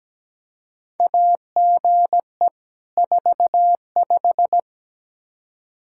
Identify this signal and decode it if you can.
Morse code, heard as a keyed tone, about 17 wpm, 700 Hz. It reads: AGE 45
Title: Practice QSO - Exchange 3 - More Info - 17wpm